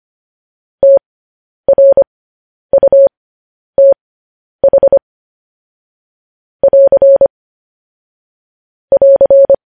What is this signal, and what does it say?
Morse code, heard as a keyed tone, about 25 wpm, spaced as for 10 wpm, 550 Hz. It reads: TRUTH <AR> <AR>
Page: news headlines in morse code at 10 wpm